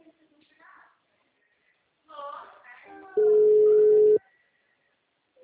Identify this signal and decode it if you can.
Ringback tone (Europe, 425 Hz).